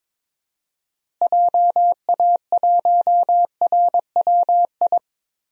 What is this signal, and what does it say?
Morse code, heard as a keyed tone, about 22 wpm, 700 Hz. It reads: JA1RWI
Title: USA Call Signs - 6 Characters - Repeated 2x - 22wpm